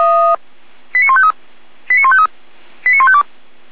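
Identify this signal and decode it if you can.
Five-tone selective call (ZVEI), heard as quick runs of five tones, tones of about 70 ms. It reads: 89153 89153 89152
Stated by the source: ZVEI five-tone selective call 89060, received on an SDR.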